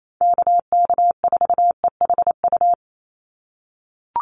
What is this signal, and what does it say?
Morse code, heard as a keyed tone, about 28 wpm, 700 Hz. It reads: KK4EHU
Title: USA Call Signs 28wpm